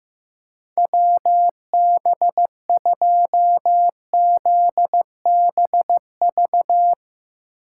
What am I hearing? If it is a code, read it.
Morse code, heard as a keyed tone, about 15 wpm, 700 Hz. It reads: WB2ZBV
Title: USA Call Signs - 6 Characters - 6x Character Spacing - 15wpm